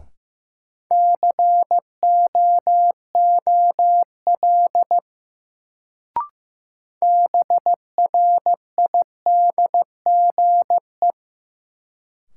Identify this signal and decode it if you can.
Morse code, heard as a keyed tone, about 15 wpm, 700 Hz. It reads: COOL BRIDGE
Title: Sentences from Top 1200 Words - New Word Drill - 15wpm